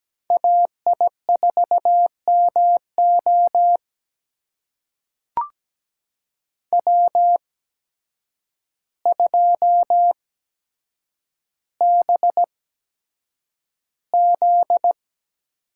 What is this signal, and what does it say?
Morse code, heard as a keyed tone, about 17 wpm, spaced as for 8 wpm, 700 Hz. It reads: AI4MO W2BZ
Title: USA Call Signs - 8x Character Spacing - 17wpm